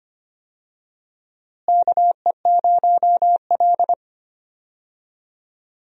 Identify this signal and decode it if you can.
Morse code, heard as a keyed tone, about 25 wpm, 700 Hz. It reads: KE0L
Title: USA Call Signs - 4 Characters - Repeated 3x - 25wpm